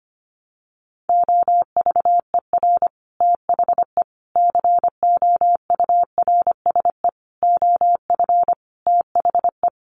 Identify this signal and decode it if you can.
Morse code, heard as a keyed tone, about 25 wpm, 700 Hz. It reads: OVER THE COURSE OF THE
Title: Sets of 5 Words 25wpm